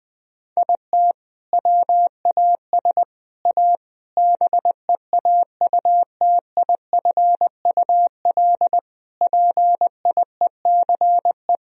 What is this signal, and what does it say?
Morse code, heard as a keyed tone, about 21 wpm, 700 Hz. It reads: IT WAS A BEAUTIFUL PIECE
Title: Sentences from Top 1000 Words - Review Part 2 - 20wpm